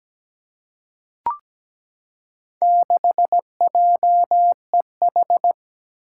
Morse code, heard as a keyed tone, about 17 wpm, 700 Hz. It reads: 6JEH
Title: Four Challenging Letter-Number Combinations 17wpm